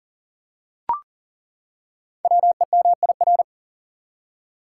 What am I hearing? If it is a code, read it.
Morse code, heard as a keyed tone, about 42 wpm, 690 Hz. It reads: WEMIR
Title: Code Groups - 5 Letters - 40wpm